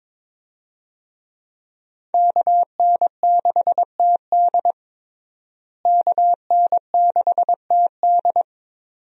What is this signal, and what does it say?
Morse code, heard as a keyed tone, about 22 wpm, 700 Hz. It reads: KN6TD KN6TD
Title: USA Call Signs - 5 Characters - Repeated 2x - 22wpm